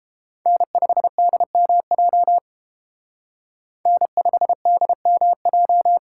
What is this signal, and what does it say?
Morse code, heard as a keyed tone, about 30 wpm, 700 Hz. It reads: N5DMJ N5DMJ